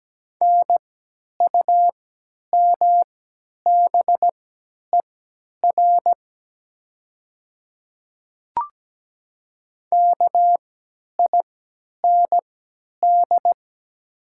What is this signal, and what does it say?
Morse code, heard as a keyed tone, about 17 wpm, spaced as for 10 wpm, 700 Hz. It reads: NUMBER KIND
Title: 3x Character Spacing - Top 200 Words - 17wpm